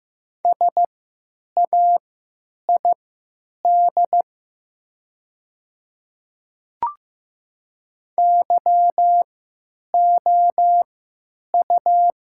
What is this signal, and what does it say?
Morse code, heard as a keyed tone, about 15 wpm, spaced as for 9 wpm, 700 Hz. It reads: SAID YOU
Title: 3x Character Spacing - Top 300 Words - 15wpm